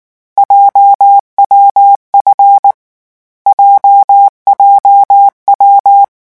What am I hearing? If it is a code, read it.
Morse code, heard as a keyed tone, about 19 wpm, 800 Hz. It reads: JWF JJW
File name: morse mp3 / lecon7-vitesse 18